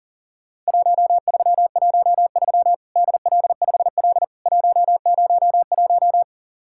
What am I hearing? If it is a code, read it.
Morse code, heard as a keyed tone, about 40 wpm, 700 Hz. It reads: 1313 DL5L 101
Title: Worked All Europe DX Contest - QTC Exchange - 40wpm